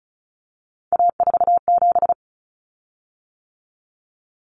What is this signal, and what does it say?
Morse code, heard as a keyed tone, about 34 wpm, 700 Hz. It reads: A47
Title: Common Cut Numbers - 3-Digit 35wpm